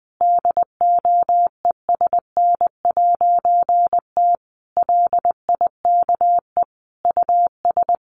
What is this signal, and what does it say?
Morse code, heard as a keyed tone, about 20 wpm, 700 Hz. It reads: DOESN'T LIKE US
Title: A Sound Of Thunder 20wpm